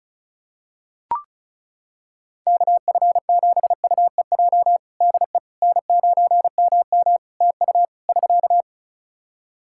Morse code, heard as a keyed tone, about 34 wpm, 700 Hz. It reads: KF7UEJ DE N9MM TU <SK>
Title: Practice QSO - Exchange 4 - Ending - 35wpm